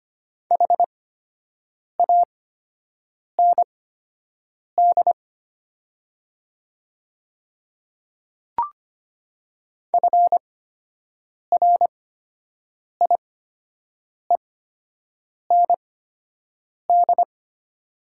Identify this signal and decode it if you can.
Morse code, heard as a keyed tone, about 25 wpm, spaced as for 7 wpm, 700 Hz. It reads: HAND FRIEND
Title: 8x Character Spacing - Top 500 Words - 25wpm